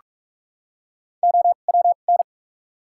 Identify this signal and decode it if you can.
Morse code, heard as a keyed tone, about 46 wpm, spaced as for 32 wpm, 700 Hz. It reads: OWN
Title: 2x Character Spacing - Top 300 Words - 45wpm